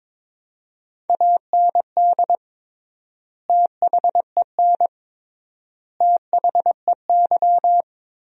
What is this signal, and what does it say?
Morse code, heard as a keyed tone, about 22 wpm, 700 Hz. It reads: AND THEN THEY
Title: 3x Word-Spacing - Sentences from Top 300 Words - 22wpm